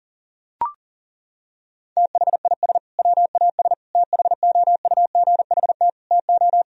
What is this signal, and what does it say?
Morse code, heard as a keyed tone, about 40 wpm, 700 Hz. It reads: THIS WAS THOUGHT TO
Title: Sentences from Top 1000 Words - Review Part 1 - 40wpm